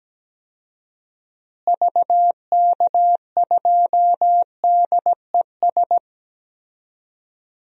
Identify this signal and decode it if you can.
Morse code, heard as a keyed tone, about 17 wpm, 700 Hz. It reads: VK2DES
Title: DXCC-100 Call Signs - 6 Characters - Repeated 2x - 17wpm